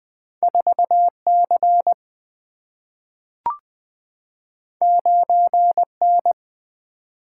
Morse code, heard as a keyed tone, about 20 wpm, 700 Hz. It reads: 4 C 9 N
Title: Single-Letter Single-Number 20wpm